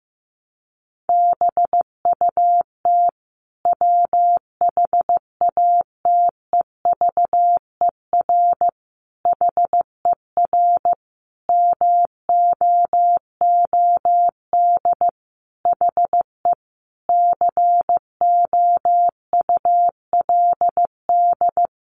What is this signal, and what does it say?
Morse code, heard as a keyed tone, about 15 wpm, 700 Hz. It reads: BUT WHATEVER HER MOOD HE COULD
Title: Sentences from Top 1100 Words 15wpm